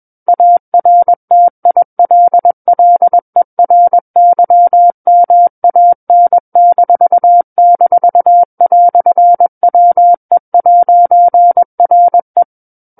Morse code, heard as a keyed tone, about 21 wpm, 700 Hz. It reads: ARTILLERYMAN--"WE'RE
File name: Book One / 12-Chapter 12 - 21 WPM